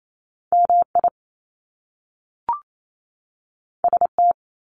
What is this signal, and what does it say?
Morse code, heard as a keyed tone, about 28 wpm, 700 Hz. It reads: MI ST
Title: Top Two Letter Combinations in Words 28wpm